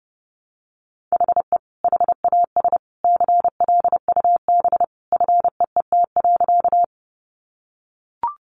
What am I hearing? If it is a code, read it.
Morse code, heard as a keyed tone, about 30 wpm, 700 Hz. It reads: HE HAS CLUB FEET.
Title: Sentences from Top 1200 Words 30wpm